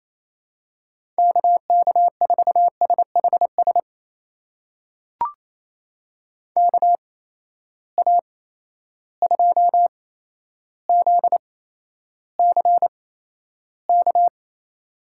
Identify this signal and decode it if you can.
Morse code, heard as a keyed tone, about 28 wpm, spaced as for 12 wpm, 700 Hz. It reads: KK4SHS KA2ZCK
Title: USA Call Signs - 6 Characters - 8x Character Spacing - 28wpm